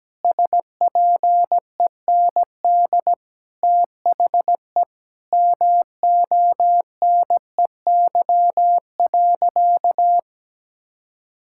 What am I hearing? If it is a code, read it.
Morse code, heard as a keyed tone, about 17 wpm, 700 Hz. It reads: SPEND THE MONEY.
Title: Sentences from Top 1000 Words - Review Part 5 - 17wpm